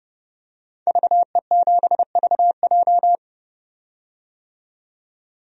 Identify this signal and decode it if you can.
Morse code, heard as a keyed tone, about 32 wpm, 700 Hz. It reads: VE7VJ
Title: DXCC-100 Call Signs - 5 Characters - 30wpm